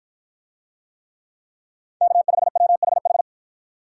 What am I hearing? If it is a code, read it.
Morse code, heard as a keyed tone, about 53 wpm, 700 Hz. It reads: K5WHH